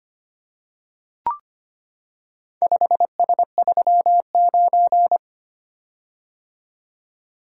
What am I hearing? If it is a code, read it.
Morse code, heard as a keyed tone, about 25 wpm, 700 Hz. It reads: 5S39
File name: practice / 25wpm / Four Letter-Number Combinations - 25wpm